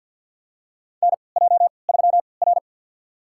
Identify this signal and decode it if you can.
Morse code, heard as a keyed tone, about 49 wpm, spaced as for 28 wpm, 700 Hz. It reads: NJ3R
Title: USA Call Signs - 4 Characters - 3x Character Spacing - 50wpm